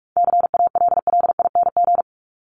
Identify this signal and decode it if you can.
Morse code, heard as a keyed tone, about 44 wpm, 700 Hz. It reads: CALLING
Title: ING Suffix - Top 100 Words with Base Words 45wpm